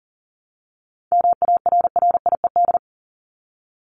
Morse code, heard as a keyed tone, about 40 wpm, 700 Hz. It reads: MARRIED
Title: ED Suffix - Top 100 Words with Base Words 40wpm